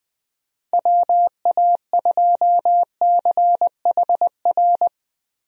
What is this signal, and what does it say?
Morse code, heard as a keyed tone, about 20 wpm, 700 Hz. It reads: WA2CHR
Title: USA Call Signs - 6 Characters - Repeated 2x - 20wpm